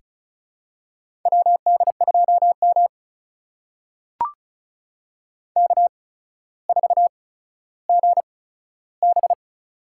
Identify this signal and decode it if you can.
Morse code, heard as a keyed tone, about 36 wpm, spaced as for 15 wpm, 700 Hz. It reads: WD2M K4GB
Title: USA Call Signs - 4 Characters - 8x Character Spacing - 35wpm